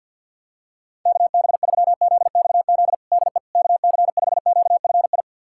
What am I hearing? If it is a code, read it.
Morse code, heard as a keyed tone, about 48 wpm, 700 Hz. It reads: KB3ZXZ DE KC5QFI
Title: USA Callsign DE Callsign - 6 Characters - 50wpm